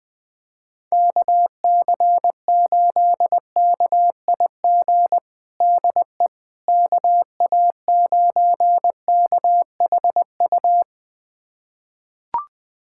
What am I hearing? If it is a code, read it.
Morse code, heard as a keyed tone, about 21 wpm, 700 Hz. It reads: KC8KIG DE KA9KHU
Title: USA Callsign DE Callsign - 6 Characters - 20wpm